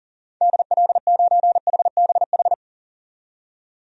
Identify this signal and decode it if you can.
Morse code, heard as a keyed tone, about 40 wpm, 700 Hz. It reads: DL9HBH
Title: DXCC-100 Call Signs - 6 Characters - 40wpm